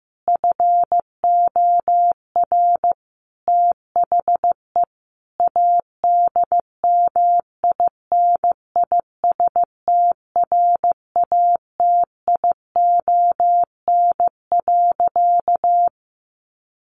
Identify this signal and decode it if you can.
Morse code, heard as a keyed tone, about 15 wpm, 700 Hz. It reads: FOR THE ADMINISTRATION.
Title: Sentences from Top 1300 Words 15wpm